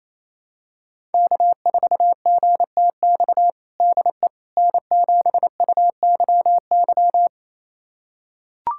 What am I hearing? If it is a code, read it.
Morse code, heard as a keyed tone, about 28 wpm, 700 Hz. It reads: K4GTX DE N7UYY
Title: USA Callsign DE Callsign 28wpm